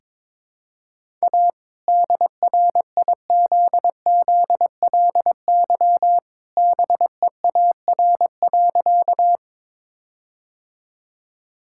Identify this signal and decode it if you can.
Morse code, heard as a keyed tone, about 22 wpm, 700 Hz. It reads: A DRIZZLY BEAR.
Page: morse-code-ninja-practice-22wpm